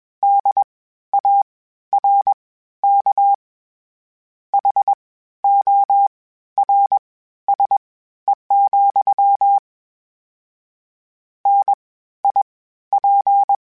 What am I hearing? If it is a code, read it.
Morse code, heard as a keyed tone, about 21 wpm, spaced as for 13 wpm, 800 Hz. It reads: DARK HORSE, NIP
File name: Misc / Misc-mp3s / WordPairs sp @20